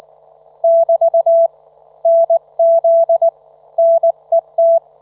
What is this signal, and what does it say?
Morse code, heard as a keyed tone, about 19 wpm, 670 Hz. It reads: <BT> NZ NET
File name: Trivia